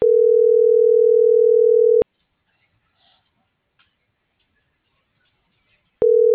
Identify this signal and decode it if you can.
Ringback tone (North America), 440 + 480 Hz.